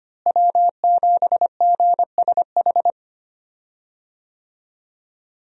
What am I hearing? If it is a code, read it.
Morse code, heard as a keyed tone, about 25 wpm, 700 Hz. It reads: W7GSH
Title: USA Call Signs - 5 Characters - Repeated 2x - 25wpm